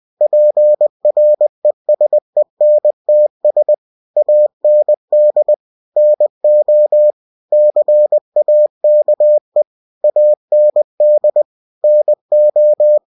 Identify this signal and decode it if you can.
Morse code, heard as a keyed tone, about 20 wpm, 600 Hz. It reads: PRESENTS AND NO CAKE AND NO
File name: cw-books / winnie-the-pooh / winnie-20wpm-page-0048